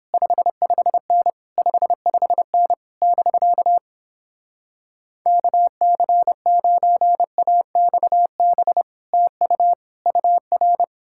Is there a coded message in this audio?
Yes: Morse code from 0.1 to 10.8 s, heard as a keyed tone, about 29 wpm, 700 Hz. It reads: 55N 55N <BK> KC9AXB TU UR